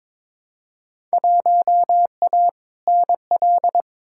Morse code, heard as a keyed tone, about 22 wpm, 700 Hz. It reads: 1A NL
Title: Field Day Exchanges Encore - 22wpm